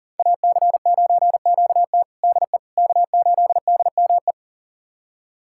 Morse code, heard as a keyed tone, about 40 wpm, 700 Hz. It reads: AC9QT DE K8DME
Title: USA Callsign DE Callsign - 5 Characters - 40wpm